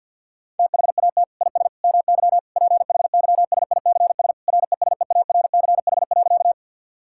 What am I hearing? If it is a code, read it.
Morse code, heard as a keyed tone, about 48 wpm, 690 Hz. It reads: THAT IS MY PHYSICS RESEARCH.
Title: Sentences from Top 1000 Words - Review Part 5 - 50wpm